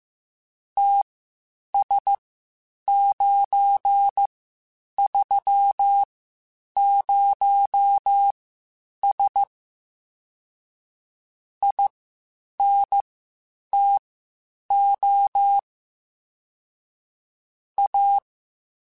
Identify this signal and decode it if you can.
Morse code, heard as a keyed tone, about 15 wpm, spaced as for 8 wpm, 800 Hz. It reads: TS930S INTO A
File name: (spaced) / QSOs Spaced-mp3s / WA7QOX 2 @13 sp